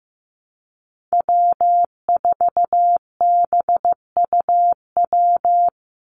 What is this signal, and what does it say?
Morse code, heard as a keyed tone, about 15 wpm, 700 Hz. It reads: W4BUW
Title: Call Signs Repeated 2x - 15wpm